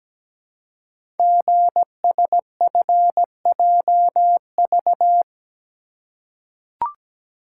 Morse code, heard as a keyed tone, about 17 wpm, 700 Hz. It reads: GSFJV
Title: Code Groups - 5 Letters - 17wpm